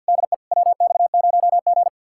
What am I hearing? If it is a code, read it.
Morse code, heard as a keyed tone, about 57 wpm, 700 Hz. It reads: DE WX0G
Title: Speed-Racing Common QSO Phrases - 50wpm